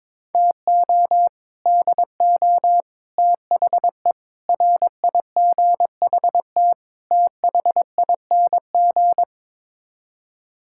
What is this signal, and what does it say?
Morse code, heard as a keyed tone, about 22 wpm, 700 Hz. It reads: TO DO THE RIGHT THING
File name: practice / 22wpm / Build A Phrase - 5-Word Phrases - 22wpm